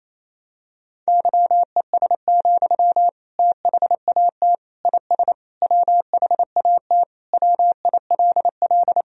Morse code, heard as a keyed tone, about 28 wpm, 700 Hz. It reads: YES, THAT IS WHAT WILL
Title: Sentences from Top 500 Words and Abbreviations 28wpm